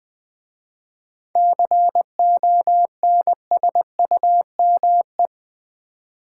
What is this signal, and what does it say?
Morse code, heard as a keyed tone, about 20 wpm, 700 Hz. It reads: CONSUME
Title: 7 Letter Words - First 3 Letters Given - 20wpm